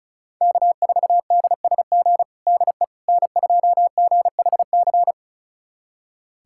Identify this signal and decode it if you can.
Morse code, heard as a keyed tone, about 34 wpm, 700 Hz. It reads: K4DSG DE N2GHC
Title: USA Callsign DE Callsign - 5 Characters - 35wpm